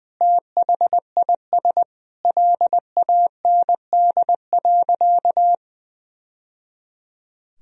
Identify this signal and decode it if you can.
Morse code, heard as a keyed tone, about 20 wpm, 700 Hz. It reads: THIS LAND.
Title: Sentences from Top 1000 Words - Review Part 4 - 20wpm